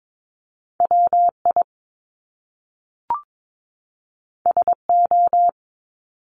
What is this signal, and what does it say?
Morse code, heard as a keyed tone, about 22 wpm, 700 Hz. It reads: WI SO
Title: Top Two Letter Combinations in Words 22wpm